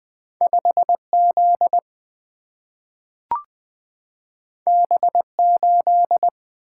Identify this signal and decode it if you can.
Morse code, heard as a keyed tone, about 20 wpm, 700 Hz. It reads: 5 Z B 8
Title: Two Letter-Number Combinations - 20wpm